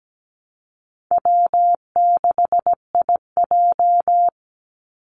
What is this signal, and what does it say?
Morse code, heard as a keyed tone, about 17 wpm, 700 Hz. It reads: W6IJ
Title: Call Signs Repeated 2x - 17wpm